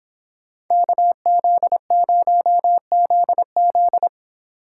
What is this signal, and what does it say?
Morse code, heard as a keyed tone, about 26 wpm, 700 Hz. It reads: KZ0ZZ